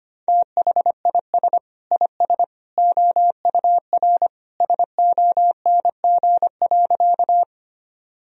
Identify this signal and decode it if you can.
Morse code, heard as a keyed tone, about 25 wpm, 700 Hz. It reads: THIS IS OUR SONG.
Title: Sentences from Top 1000 Words - Review Part 2 - 25wpm